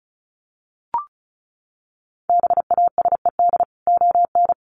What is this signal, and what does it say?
Morse code, heard as a keyed tone, about 34 wpm, 700 Hz. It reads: BASED ON
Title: Sets of 2 Words 35wpm